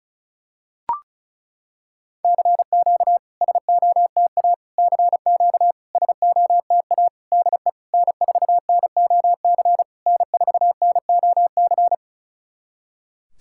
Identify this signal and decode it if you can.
Morse code, heard as a keyed tone, about 36 wpm, 700 Hz. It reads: CQ SOTA CQ SOTA DE N4NOC N4NOC
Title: SOTA QSO Practice 35wpm